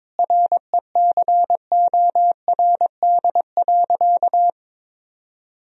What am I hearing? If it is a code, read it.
Morse code, heard as a keyed tone, about 22 wpm, 700 Hz. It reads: RECORD.
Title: Sentences from Top 1000 Words - Review Part 3 - 22wpm